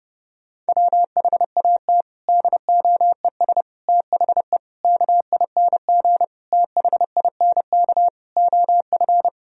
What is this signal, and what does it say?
Morse code, heard as a keyed tone, about 30 wpm, 700 Hz. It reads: WHAT DOES THE KING THINK OF
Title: Sentences from Top 1000 Words - Review Part 2 - 30wpm